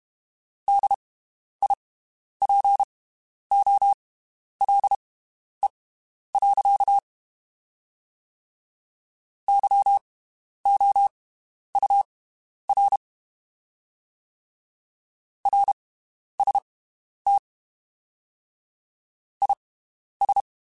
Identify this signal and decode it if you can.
Morse code, heard as a keyed tone, about 32 wpm, spaced as for 11 wpm, 800 Hz. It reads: DIPOLE. YOUR RST IS
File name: (spaced) / QSOs Spaced-mp3s / WD0FBT 1 @30 sp